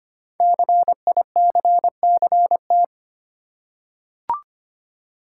Morse code, heard as a keyed tone, about 25 wpm, 700 Hz. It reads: CICCT
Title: Code Groups - 5 Letters - 25wpm